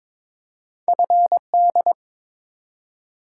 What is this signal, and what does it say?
Morse code, heard as a keyed tone, about 23 wpm, 700 Hz. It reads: FD
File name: practice / 22wpm / Two Letter-Number Combinations - 22wpm